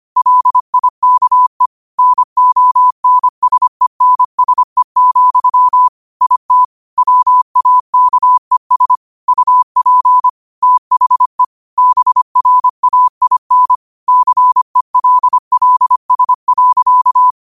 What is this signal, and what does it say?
Morse code, heard as a keyed tone, about 25 wpm, 1000 Hz. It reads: LIKE NONSENSE, IT WAKES UP THE BRAIN CELLS.
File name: qotdfiles / 25wpm